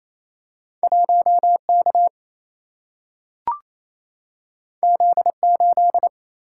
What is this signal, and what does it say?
Morse code, heard as a keyed tone, about 28 wpm, 700 Hz. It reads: 1K Z8
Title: Single-Letter Single-Number 28wpm